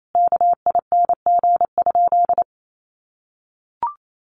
Morse code, heard as a keyed tone, about 28 wpm, 700 Hz. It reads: KING?
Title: Sentences from Top 1400 Words 28wpm